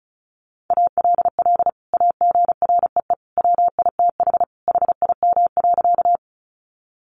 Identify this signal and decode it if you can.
Morse code, heard as a keyed tone, about 34 wpm, 700 Hz. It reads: ALL AGREE WITH HIM.
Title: Sentences from Top 1100 Words 35wpm